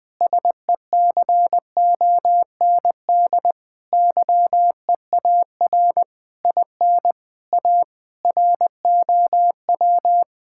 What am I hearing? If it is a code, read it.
Morse code, heard as a keyed tone, about 20 wpm, 700 Hz. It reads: SECOND YEAR IN A ROW
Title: Sets of 8 Words - 20wpm